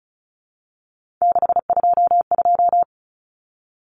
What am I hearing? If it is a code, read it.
Morse code, heard as a keyed tone, about 34 wpm, 700 Hz. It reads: 622
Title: Common Cut Numbers - 3-Digit 35wpm